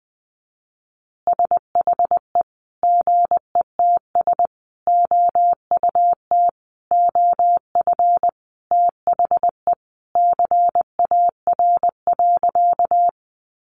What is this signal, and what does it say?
Morse code, heard as a keyed tone, about 19 wpm, 700 Hz. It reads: SHE GETS OUT OF THE CAR.
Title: Easy Phrases 20wpm